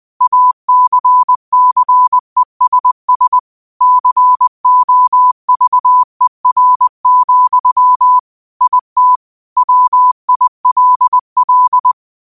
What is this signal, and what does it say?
Morse code, heard as a keyed tone, about 20 wpm, 1000 Hz. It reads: ACCESS COVER, IT WILL
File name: qotdfiles / 20wpm